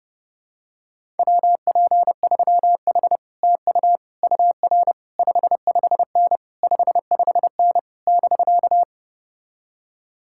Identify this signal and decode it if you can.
Morse code, heard as a keyed tone, about 31 wpm, 700 Hz. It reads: WP3H TU UR 55N 55N <BK>